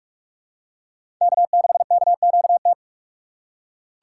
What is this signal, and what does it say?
Morse code, heard as a keyed tone, about 45 wpm, 690 Hz. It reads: K6KQT
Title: USA Call Signs - 5 Characters - 6x Character Spacing - 45wpm